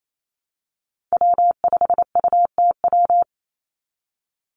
Morse code, heard as a keyed tone, about 28 wpm, 700 Hz. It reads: W5UTW
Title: Call Signs - 28wpm